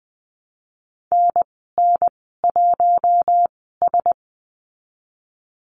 Morse code, heard as a keyed tone, about 20 wpm, spaced as for 15 wpm, 700 Hz. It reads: NN1S
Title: 055 Lesson 10g - Letter r - Call Signs 20wpm